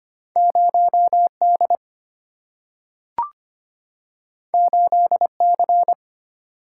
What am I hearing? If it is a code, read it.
Morse code, heard as a keyed tone, about 25 wpm, 690 Hz. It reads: 0D 8C
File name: Two Letter-Number Combinations - 25wpm